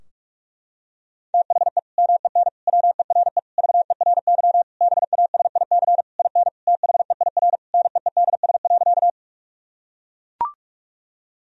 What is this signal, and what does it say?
Morse code, heard as a keyed tone, about 41 wpm, 700 Hz. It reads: THE MEN WERE VERY BASIC IN THEIR NEEDS.
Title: Sentences from Top 1000 Words - Review Part 3 - 45wpm